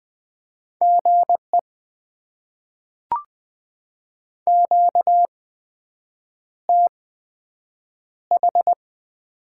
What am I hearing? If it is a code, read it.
Morse code, heard as a keyed tone, about 20 wpm, spaced as for 7 wpm, 700 Hz. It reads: GE QTH
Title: Top 20 QSO Elements - 8x Character Spacing - 20wpm